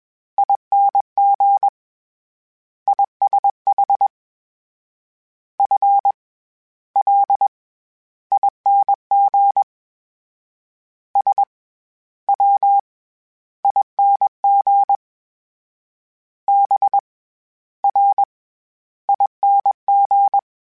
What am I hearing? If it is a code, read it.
Morse code, heard as a keyed tone, about 21 wpm, 800 Hz. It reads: ING ISH F L ING S W ING B R ING